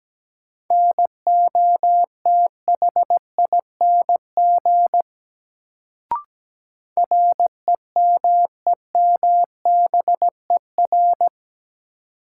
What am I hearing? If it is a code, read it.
Morse code, heard as a keyed tone, about 17 wpm, 700 Hz. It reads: NOTHING REMEMBER
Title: Sets of 4 Words - New Word Drill - 17wpm